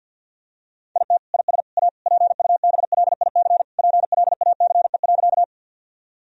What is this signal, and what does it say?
Morse code, heard as a keyed tone, about 50 wpm, 700 Hz. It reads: IT IS A PUBLIC PLACE.
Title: Sentences from Top 1000 Words - Review Part 3 - 50wpm